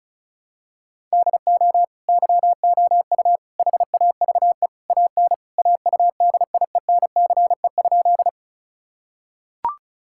Morse code, heard as a keyed tone, about 36 wpm, 700 Hz. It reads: DO YOU HAVE AN AUDIENCE?
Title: Sentences from Top 1000 Words - Review Part 5 - 35wpm